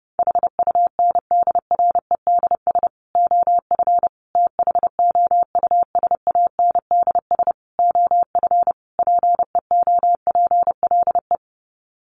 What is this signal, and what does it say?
Morse code, heard as a keyed tone, about 30 wpm, 700 Hz. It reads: HUNDREDS OF THOUSANDS OF PEOPLE
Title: Sets of 5 Words Encore2 30wpm